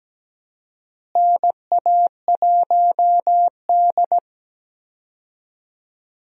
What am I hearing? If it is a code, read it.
Morse code, heard as a keyed tone, about 17 wpm, 700 Hz. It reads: NA1D
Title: USA Call Signs - 4 Characters - Repeated 3x - 17wpm